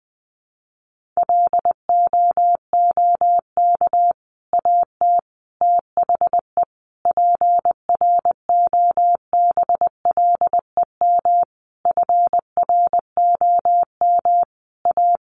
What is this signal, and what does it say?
Morse code, heard as a keyed tone, about 19 wpm, 700 Hz. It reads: LOOK AT THE PROBLEM FROM A
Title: Sentences from Top 1500 Words 20wpm